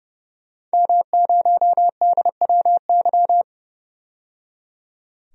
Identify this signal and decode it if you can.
Morse code, heard as a keyed tone, about 30 wpm, 700 Hz. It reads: M0DWY
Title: DXCC-100 Call Signs - 5 Characters - 30wpm